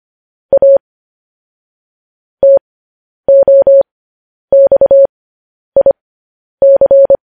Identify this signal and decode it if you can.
Morse code, heard as a keyed tone, about 25 wpm, spaced as for 10 wpm, 550 Hz. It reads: A TOXIC
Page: news-headlines-in-morse-code-at-10-wpm